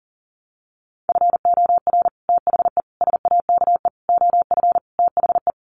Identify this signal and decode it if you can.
Morse code, heard as a keyed tone, about 40 wpm, 700 Hz. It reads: FOR THE SAKE OF THE
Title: Sets of 5 Words Encore 40wpm